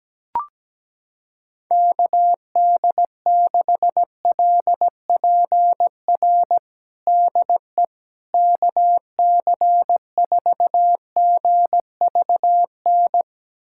Morse code, heard as a keyed tone, about 17 wpm, 700 Hz. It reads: KD6LPR DE KC4GVN
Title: USA Callsign DE Callsign - 6 Characters - 17wpm